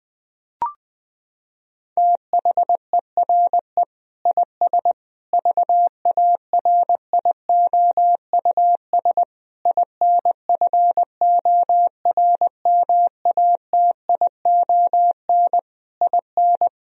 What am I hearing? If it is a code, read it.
Morse code, heard as a keyed tone, about 21 wpm, 700 Hz. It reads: THERE IS VARIOUS INFORMATION IN
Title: Sentences from Top 1000 Words - Review Part 4 - 20wpm